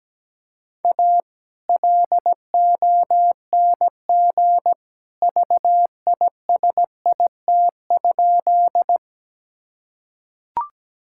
Morse code, heard as a keyed tone, about 17 wpm, 700 Hz. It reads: A LONG VISIT?
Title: Sentences from Top 1000 Words - Review Part 6 - 17wpm